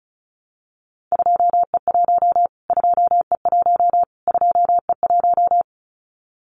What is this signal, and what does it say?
Morse code, heard as a keyed tone, about 34 wpm, 700 Hz. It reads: 2E1 2E1 2E1
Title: Callsign Prefixes for Top 25 DXCC - Repeated 3x - 35wpm